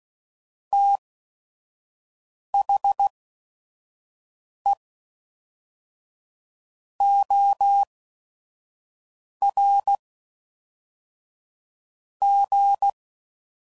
Morse code, heard as a keyed tone, about 16 wpm, spaced as for 5 wpm, 800 Hz. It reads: THE OR G
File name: CW Code Groups @5WPM / Technical Subjects / AND Gate